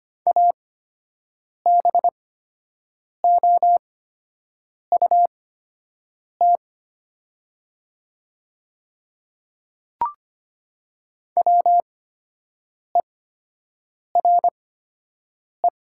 Morse code, heard as a keyed tone, about 25 wpm, spaced as for 7 wpm, 700 Hz. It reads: ABOUT WERE
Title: 8x Character Spacing - Top 100 Words - 25wpm